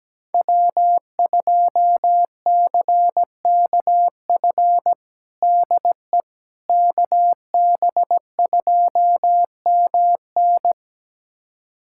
Morse code, heard as a keyed tone, about 17 wpm, 700 Hz. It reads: W2CKF DE KB2MN
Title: USA Callsign DE Callsign - 5 Characters - 17wpm